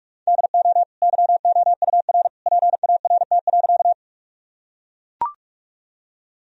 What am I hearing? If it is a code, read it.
Morse code, heard as a keyed tone, about 44 wpm, 700 Hz. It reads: DO YOUR PART.
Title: Sentences from Top 1000 Words - Review Part 1 - 45wpm